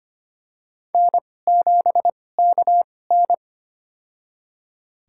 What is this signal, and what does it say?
Morse code, heard as a keyed tone, about 25 wpm, spaced as for 18 wpm, 700 Hz. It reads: N7KN
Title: USA Call Signs - 4 Characters - 2x Character Spacing - 25wpm